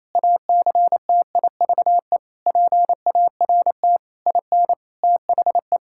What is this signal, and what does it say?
Morse code, heard as a keyed tone, about 28 wpm, 700 Hz. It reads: ACTIVE PART IN THE
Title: Sentences from Top 1000 Words - Review Part 6 - 28wpm